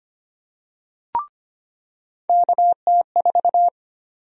Morse code, heard as a keyed tone, about 25 wpm, 700 Hz. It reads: KT4
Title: USA Call Sign Prefixes - 25wpm